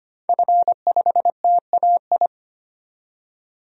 Morse code, heard as a keyed tone, about 25 wpm, 700 Hz. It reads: F5TAI
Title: DXCC-100 Call Signs - 5 Characters - 25wpm